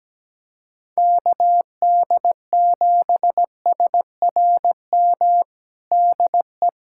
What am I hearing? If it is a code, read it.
Morse code, heard as a keyed tone, about 17 wpm, 700 Hz. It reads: KD7SRM DE
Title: USA Callsign DE Callsign 17wpm